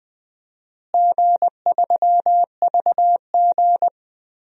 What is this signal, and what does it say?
Morse code, heard as a keyed tone, about 21 wpm, 700 Hz. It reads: G3VG
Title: DXCC-100 Call Signs - 4 Characters - Repeated 3x - 20wpm